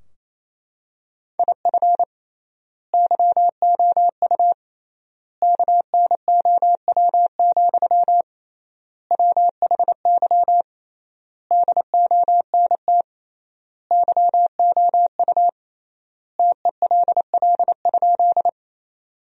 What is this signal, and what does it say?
Morse code, heard as a keyed tone, about 27 wpm, 700 Hz. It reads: IF YOU KNOW, WHY DONT YOU TELL?
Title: 3x Word-Spacing - Sentences from Top 300 Words - 28wpm